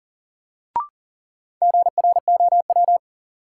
Morse code, heard as a keyed tone, about 40 wpm, 700 Hz. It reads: GROW
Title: Sentences from Top 300 Words - New Word Drill - 40wpm